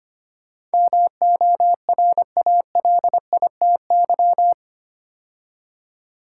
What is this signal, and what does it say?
Morse code, heard as a keyed tone, about 25 wpm, 700 Hz. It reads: MORALITY
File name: 8 Letter Words - First 3 Letters Given - 25wpm